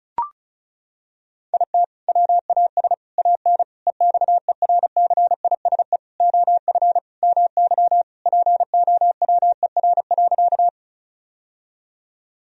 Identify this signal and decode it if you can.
Morse code, heard as a keyed tone, about 34 wpm, 700 Hz. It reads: IT WAS AN EXERCISE OF MY POWER.
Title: Sentences from Top 1000 Words - Review Part 4 - 35wpm